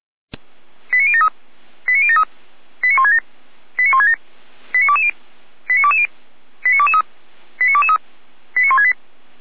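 Five-tone selective call (ZVEI), heard as quick runs of five tones, tones of about 70 ms. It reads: 89073 89073 89167 89167 89220 89220 89223 89223 89168
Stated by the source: ZVEI five-tone selective call 89060 received by SDR.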